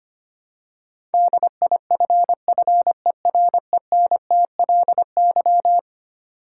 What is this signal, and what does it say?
Morse code, heard as a keyed tone, about 25 wpm, 700 Hz. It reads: DIFFERENTLY
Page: morse-code-ninja-practice-25wpm